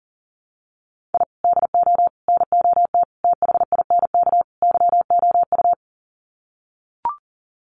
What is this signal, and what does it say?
Morse code, heard as a keyed tone, about 40 wpm, 700 Hz. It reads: I DO NOT THINK YOU
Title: Sets of 5 Words 40wpm